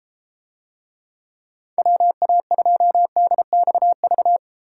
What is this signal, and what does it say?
Morse code, heard as a keyed tone, about 34 wpm, 700 Hz. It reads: WA2DXV